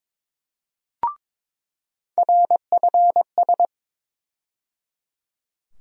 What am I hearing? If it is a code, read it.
Morse code, heard as a keyed tone, about 21 wpm, 700 Hz. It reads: RFS
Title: USA Call Sign Postfixes - 3 Letters 22wpm